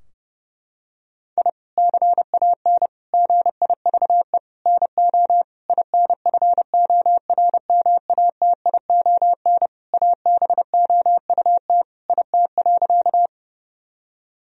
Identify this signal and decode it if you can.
Morse code, heard as a keyed tone, about 30 wpm, 700 Hz. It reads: I CAN GIVE NO INFORMATION ABOUT IT.
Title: Sentences from Top 1000 Words - Review Part 3 - 30wpm